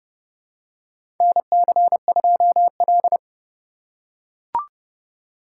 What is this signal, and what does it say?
Morse code, heard as a keyed tone, about 29 wpm, 700 Hz. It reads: NC2L
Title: USA Call Signs - 4 Characters - 6x Character Spacing - 30wpm